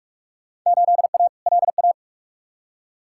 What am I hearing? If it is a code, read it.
Morse code, heard as a keyed tone, about 46 wpm, 700 Hz. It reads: 8A LA
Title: Field Day Exchanges Encore - 45wpm